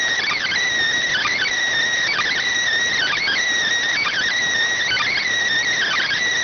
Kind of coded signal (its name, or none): Emergency Alert System (SAME) tones